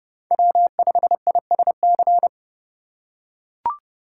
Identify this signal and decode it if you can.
Morse code, heard as a keyed tone, about 28 wpm, 700 Hz. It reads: W5ISC